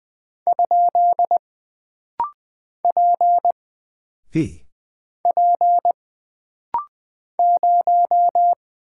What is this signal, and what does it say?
Morse code, heard as a keyed tone, about 20 wpm, 700 Hz. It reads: ? P P 0
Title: Single Character (Letter:Number:Punctuation:Prosign) - Rapid-Fire - 20wpm